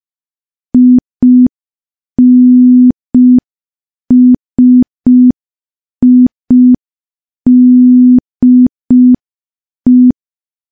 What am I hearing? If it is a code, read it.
Morse code, heard as a keyed tone, about 5 wpm, 250 Hz. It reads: INSIDE
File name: morse / morse-training-05WPM